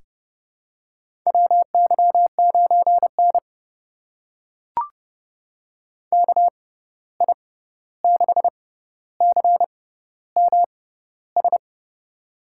Morse code, heard as a keyed tone, about 30 wpm, spaced as for 14 wpm, 700 Hz. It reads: WY9N KI6CMS
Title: USA Call Signs - 6x Character Spacing - 30wpm